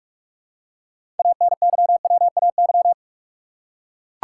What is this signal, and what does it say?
Morse code, heard as a keyed tone, about 44 wpm, 700 Hz. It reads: ANYWAY
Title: Six Letter Words - 45wpm